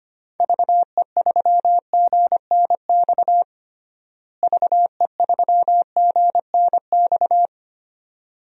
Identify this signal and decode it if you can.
Morse code, heard as a keyed tone, about 25 wpm, 700 Hz. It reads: VE3GNX VE3GNX
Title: DXCC-100 Call Signs - 6 Characters - Repeated 2x - 25wpm